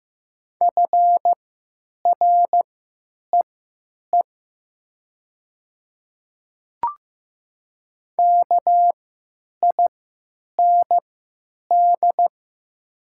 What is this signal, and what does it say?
Morse code, heard as a keyed tone, about 15 wpm, spaced as for 9 wpm, 700 Hz. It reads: FREE KIND
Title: 3x Character Spacing - Top 500 Words - 15wpm